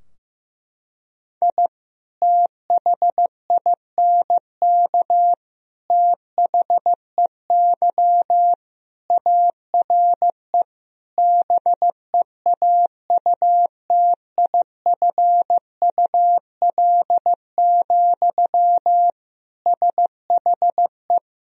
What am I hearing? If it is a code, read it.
Morse code, heard as a keyed tone, about 15 wpm, 700 Hz. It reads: I THINK THEY ARE BEAUTIFUL, SHE
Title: Sentences from Top 1000 Words - Review Part 4 - 15wpm